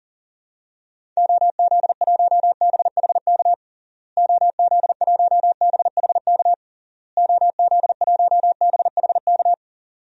Morse code, heard as a keyed tone, about 40 wpm, 700 Hz. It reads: OZ1BHK OZ1BHK OZ1BHK
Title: DXCC-100 Call Signs - 6 Characters - Repeated 3x - 40wpm